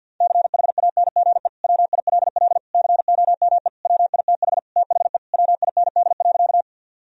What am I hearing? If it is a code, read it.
Morse code, heard as a keyed tone, about 51 wpm, 700 Hz. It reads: CHANGE WILL COME WITH THE WIND.
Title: Sentences from Top 1000 Words - Review Part 2 - 50wpm